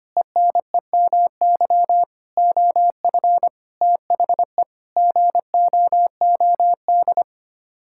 Morse code, heard as a keyed tone, about 25 wpm, 700 Hz. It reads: ENEMY OF THE GOOD
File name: Sets of 9 Words - 25wpm